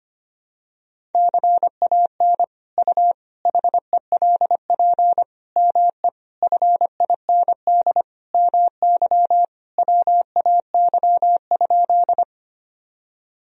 Morse code, heard as a keyed tone, about 25 wpm, 700 Hz. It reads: CAN U HELP ME FIND MY WAY?
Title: Sentences from Top 200 Words and Abbreviations 25wpm